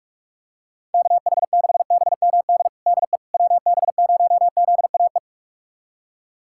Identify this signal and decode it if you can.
Morse code, heard as a keyed tone, about 41 wpm, 700 Hz. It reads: KH6BMD DE WB0ZAE
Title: USA Callsign DE Callsign - 6 Characters - 45wpm